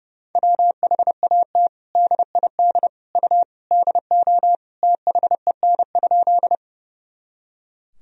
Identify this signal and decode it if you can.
Morse code, heard as a keyed tone, about 30 wpm, 700 Hz. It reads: WHAT DID U DO THEN?
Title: Sentences from Top 100 Words and Abbreviations 30wpm